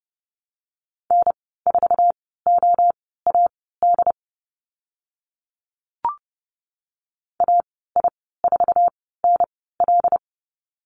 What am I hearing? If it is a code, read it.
Morse code, heard as a keyed tone, about 30 wpm, spaced as for 17 wpm, 700 Hz. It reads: N4OAD AI4NL
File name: Lesson 15g - Number 5 - Call Signs 30wpm